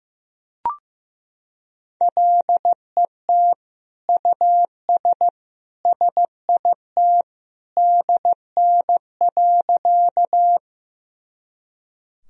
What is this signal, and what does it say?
Morse code, heard as a keyed tone, about 16 wpm, 700 Hz. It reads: LET US SIT DN.
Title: Sentences from Top 500 Words and Abbreviations 15wpm